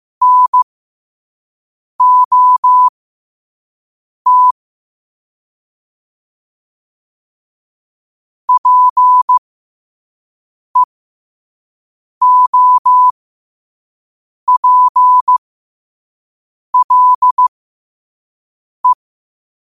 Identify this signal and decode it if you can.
Morse code, heard as a keyed tone, about 15 wpm, spaced as for 5 wpm, 1000 Hz. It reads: NOT PEOPLE
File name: qotdfiles / 5wpm